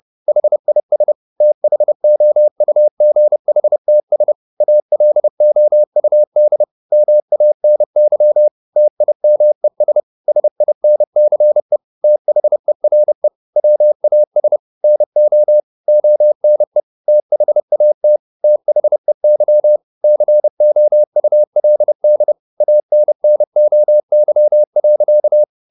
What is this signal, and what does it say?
Morse code, heard as a keyed tone, about 30 wpm, 600 Hz. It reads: HIS THOUGHTS ALOUD MANY TIMES SINCE THERE WAS NO ONE THAT THEY COULD ANNOY.
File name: the old man and the sea / the old man and the sea 1 30wpm